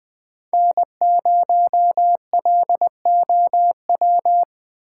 Morse code, heard as a keyed tone, about 20 wpm, 700 Hz. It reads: N0LOW
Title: USA Call Signs - 6x Character Spacing - 20wpm